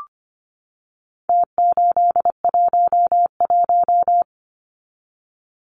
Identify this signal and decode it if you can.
Morse code, heard as a keyed tone, about 25 wpm, 700 Hz. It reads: T811
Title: Numbers - Contest Style - 4-Digit 25wpm